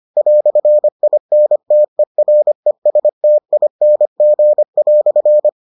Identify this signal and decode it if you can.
Morse code, heard as a keyed tone, about 25 wpm, 600 Hz. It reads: "INTERESTING"